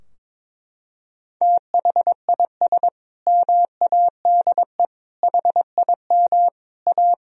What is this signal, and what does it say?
Morse code, heard as a keyed tone, about 22 wpm, 700 Hz. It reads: THIS MADE HIM A
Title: Sentences from Top 1000 Words - Review Part 3 - 22wpm